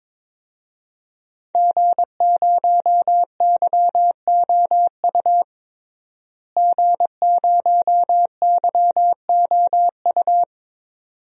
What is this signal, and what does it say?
Morse code, heard as a keyed tone, about 22 wpm, 700 Hz. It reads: G0YOU G0YOU
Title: DXCC-100 Call Signs - 5 Characters - Repeated 2x - 22wpm